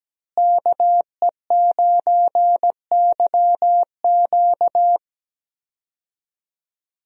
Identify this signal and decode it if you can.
Morse code, heard as a keyed tone, about 17 wpm, 700 Hz. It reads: KE9YQ
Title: USA Call Signs - 5 Characters - Repeated 2x - 17wpm